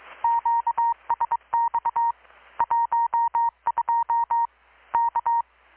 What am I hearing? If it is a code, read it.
Morse code, heard as a keyed tone, about 21 wpm, 930 Hz. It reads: QSX 12 K
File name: blog / swl / 6-abril-2019-12695-USB-KPS